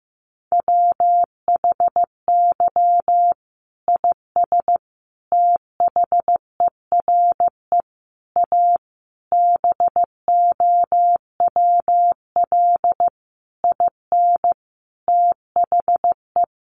Morse code, heard as a keyed tone, about 14 wpm, 700 Hz. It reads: WHY IS THERE A BOWL IN THE
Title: Sentences from Top 1200 Words 15wpm